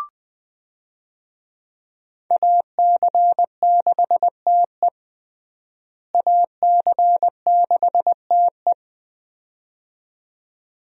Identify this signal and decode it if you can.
Morse code, heard as a keyed tone, about 21 wpm, 700 Hz. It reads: AC6TE AC6TE
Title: USA Call Signs - 5 Characters - Repeated 2x - 20wpm